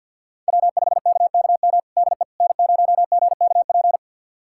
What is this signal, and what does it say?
Morse code, heard as a keyed tone, about 49 wpm, 700 Hz. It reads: W5KKM DE N0GKP
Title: USA Callsign DE Callsign - 5 Characters - 50wpm